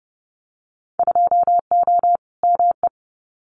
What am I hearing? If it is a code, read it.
Morse code, heard as a keyed tone, about 30 wpm, 700 Hz. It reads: 2O ME
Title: Winter Field Day Exchanges Encore 30wpm